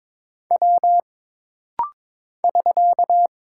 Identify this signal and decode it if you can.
Morse code, heard as a keyed tone, about 22 wpm, 700 Hz. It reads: W <SK>
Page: This rapid-fire timing is geared to push your ICR (Instant Character Recognition) skills.